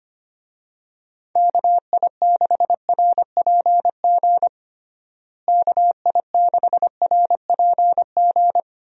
Morse code, heard as a keyed tone, about 25 wpm, 700 Hz. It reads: KI6RPG KI6RPG
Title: USA Call Signs - 6 Characters - Repeated 2x - 25wpm